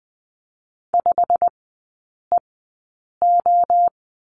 Morse code, heard as a keyed tone, about 20 wpm, spaced as for 8 wpm, 700 Hz. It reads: 5EO
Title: 100 Lesson 17e - Letter u - Three Characters 20wpm